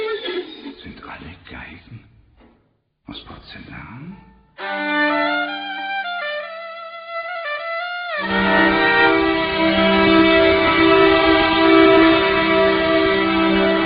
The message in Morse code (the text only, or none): none